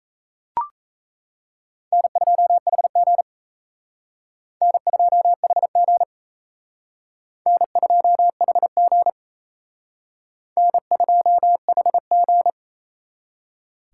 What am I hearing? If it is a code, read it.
Morse code, heard as a keyed tone, about 35 wpm, 700 Hz. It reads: N2HG N2HG N2HG N2HG
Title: Speed Racing - USA 1x2 Call Signs 28wpm